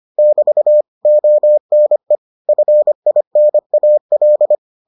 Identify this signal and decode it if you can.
Morse code, heard as a keyed tone, about 25 wpm, 600 Hz. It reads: <BT> ONE FINAL